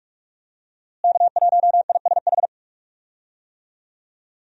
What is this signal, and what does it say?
Morse code, heard as a keyed tone, about 41 wpm, 700 Hz. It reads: K1ISH
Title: USA Call Signs - 5 Characters - Repeated 3x - 45wpm